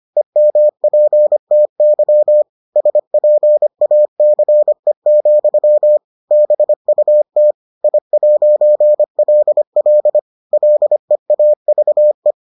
Morse code, heard as a keyed tone, about 25 wpm, 600 Hz. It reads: EMPTY SPACE, BUT I'LL LEAVE